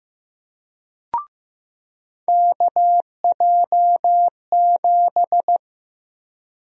Morse code, heard as a keyed tone, about 15 wpm, 700 Hz. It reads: KJ7
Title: USA Call Sign Prefixes - 15wpm